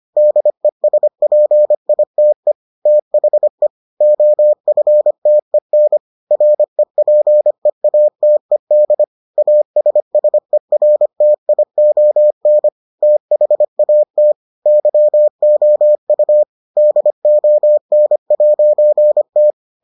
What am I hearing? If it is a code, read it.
Morse code, heard as a keyed tone, about 25 wpm, 600 Hz. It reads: DESPITE THE OFTEN REPEATED ASSERTION THAT YOU DON'T